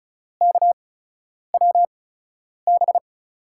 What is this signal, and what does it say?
Morse code, heard as a keyed tone, about 36 wpm, spaced as for 10 wpm, 700 Hz. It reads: KWB
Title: USA Call Signs - 5 Characters - 8x Character Spacing - 35wpm